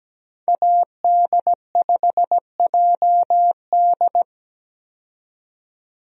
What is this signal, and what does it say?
Morse code, heard as a keyed tone, about 17 wpm, 700 Hz. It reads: AD5JD
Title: USA Call Signs - 5 Characters - 17wpm